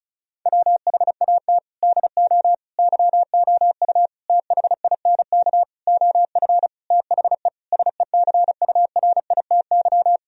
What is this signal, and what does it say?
Morse code, heard as a keyed tone, about 34 wpm, 700 Hz. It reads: WHAT DO YOU THINK OF THE SECURITY
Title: Sentences from Top 1000 Words - Review Part 4 - 35wpm